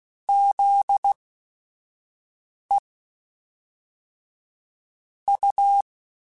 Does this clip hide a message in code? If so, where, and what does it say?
Yes: Morse code from 0.3 to 5.8 s, heard as a keyed tone, about 16 wpm, spaced as for 5 wpm, 800 Hz. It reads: ZE U